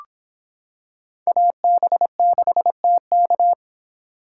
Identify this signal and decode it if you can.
Morse code, heard as a keyed tone, about 25 wpm, 700 Hz. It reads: AB6TK